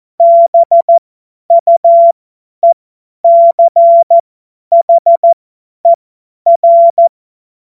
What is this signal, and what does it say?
Morse code, heard as a keyed tone, about 14 wpm, spaced as for 10 wpm, 680 Hz. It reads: BUECHER
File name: Morsekurs / Witze / Witz 09. November 2021, 50 F70